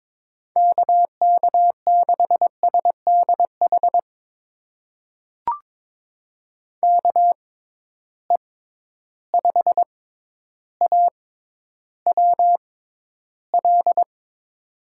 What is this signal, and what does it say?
Morse code, heard as a keyed tone, about 22 wpm, spaced as for 11 wpm, 700 Hz. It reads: KK6SDH KE5AWL
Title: USA Call Signs - 6 Characters - 6x Character Spacing - 22wpm